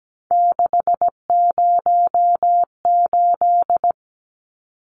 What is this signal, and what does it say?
Morse code, heard as a keyed tone, about 17 wpm, 700 Hz. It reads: 608
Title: Numbers - 3-Digit 17wpm